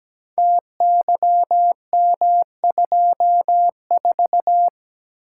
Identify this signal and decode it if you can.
Morse code, heard as a keyed tone, about 17 wpm, 700 Hz. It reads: TYM24
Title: Code Groups - 5 Letter:Numbers - 17wpm